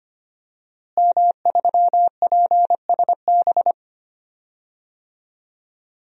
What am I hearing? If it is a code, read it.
Morse code, heard as a keyed tone, about 25 wpm, 700 Hz. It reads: M3PSB
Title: DXCC-100 Call Signs - 5 Characters - Repeated 3x - 25wpm